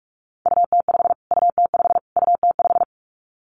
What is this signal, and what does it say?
Morse code, heard as a keyed tone, about 44 wpm, 700 Hz. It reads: UT5 UT5 UT5
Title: Callsign Prefixes for Top 25 DXCC - Repeated 3x - 45wpm